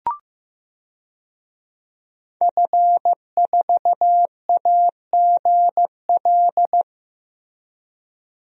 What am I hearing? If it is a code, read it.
Morse code, heard as a keyed tone, about 15 wpm, 700 Hz. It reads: F4AGL
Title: DXCC-100 Call Signs - 5 Characters - 15wpm